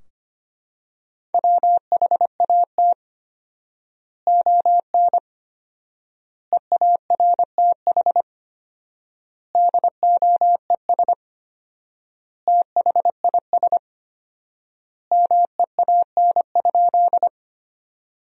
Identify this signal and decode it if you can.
Morse code, heard as a keyed tone, about 25 wpm, 700 Hz. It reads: WHAT ON EARTH DOES THIS MEAN?
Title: 4x Word-Spacing - Sentences from Top 200 Words - 25wpm